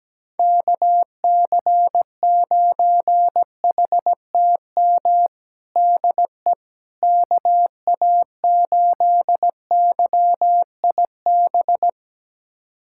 Morse code, heard as a keyed tone, about 17 wpm, 700 Hz. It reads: KC9HTM DE KA8YIB
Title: USA Callsign DE Callsign 17wpm